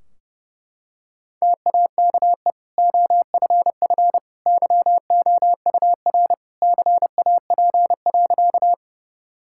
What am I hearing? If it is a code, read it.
Morse code, heard as a keyed tone, about 30 wpm, 700 Hz. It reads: TAKE OFF YOUR CAP.
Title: Sentences from Top 1600 Words 30wpm